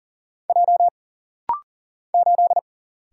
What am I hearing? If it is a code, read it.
Morse code, heard as a keyed tone, about 40 wpm, 700 Hz. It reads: J 8
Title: Single Letter-Number - Rapid-Fire 40wpm